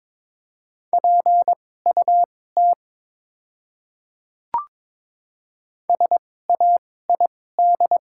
Morse code, heard as a keyed tone, about 23 wpm, spaced as for 16 wpm, 700 Hz. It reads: PUT SAID
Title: 2x Character Spacing - Top 200 Words - 22wpm